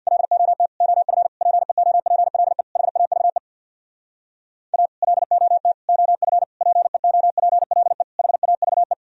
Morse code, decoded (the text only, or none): LOT OF PEOPLE HAVE A LOT OF PEOPLE HAVE